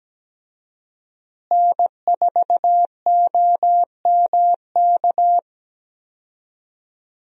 Morse code, heard as a keyed tone, about 17 wpm, 700 Hz. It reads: N4OMK
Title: USA Call Signs - 5 Characters - Repeated 3x - 17wpm